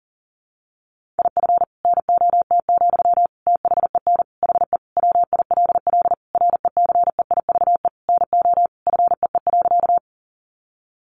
Morse code, heard as a keyed tone, about 40 wpm, 700 Hz. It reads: IF NOT, THEN HE WILL RECEIVE NO FEE.
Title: Sentences from Top 1500 Words 40wpm